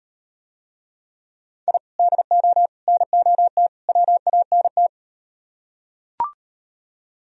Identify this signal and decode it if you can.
Morse code, heard as a keyed tone, about 38 wpm, 700 Hz. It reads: I DO NOT WANT